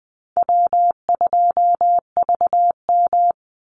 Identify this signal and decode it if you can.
Morse code, heard as a keyed tone, about 20 wpm, 700 Hz. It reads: W2VM
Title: Easy Call Signs - 20wpm